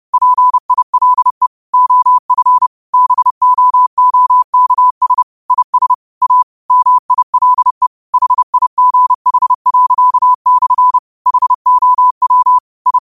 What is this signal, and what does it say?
Morse code, heard as a keyed tone, about 30 wpm, 1000 Hz. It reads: PILE OF BOOKS IS A MILE HIGH./ HOW I
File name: qotdfiles / 30wpm